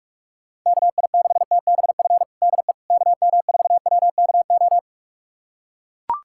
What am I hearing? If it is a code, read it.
Morse code, heard as a keyed tone, about 41 wpm, 700 Hz. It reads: KI6TBF DE KM4WKO
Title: USA Callsign DE Callsign 45wpm